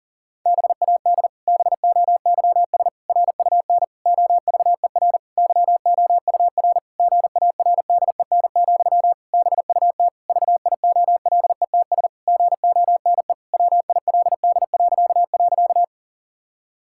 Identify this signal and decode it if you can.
Morse code, heard as a keyed tone, about 40 wpm, 700 Hz. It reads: BAD BOYS RUN OVER YOUR GARDEN, BUT VIOLETS GONE WILD..
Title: Pangram Challenge v2 - 40wpm